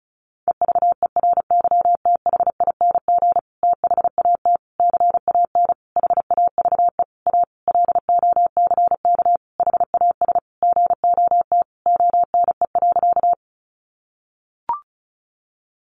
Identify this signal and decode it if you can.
Morse code, heard as a keyed tone, about 34 wpm, 700 Hz. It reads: EVERYTHING THAT CAN HAVE A LOCK HAS GOT ONE.
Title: Sentences from Top 1300 Words 35wpm